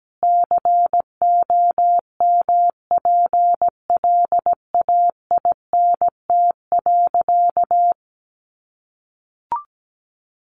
Morse code, heard as a keyed tone, about 17 wpm, 700 Hz. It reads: COMPLAINT.
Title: Sentences from Top 1400 Words 17wpm